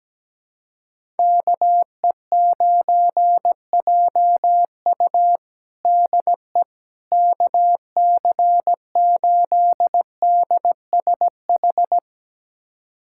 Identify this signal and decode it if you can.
Morse code, heard as a keyed tone, about 17 wpm, 700 Hz. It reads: KE9JU DE KC8DSH
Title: USA Callsign DE Callsign 17wpm